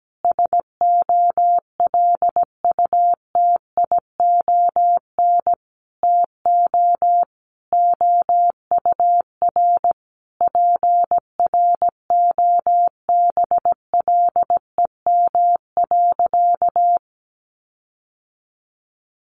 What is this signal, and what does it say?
Morse code, heard as a keyed tone, about 17 wpm, 700 Hz. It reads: SOLUTION TO OUR PROBLEM.
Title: Sentences from Top 1300 Words 17wpm